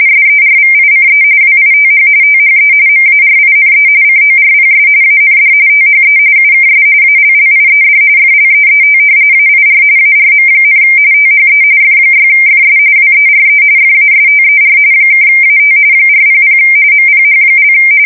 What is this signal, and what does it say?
RTTY, about 45.45 baud. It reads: SERVICE WITH THE MARINE CORPS AND THE UKS ALLIED FLEET AIR ARM, THE F6FS WERE FASTER (AT 400 MPH)
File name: fsk170 / usnavy4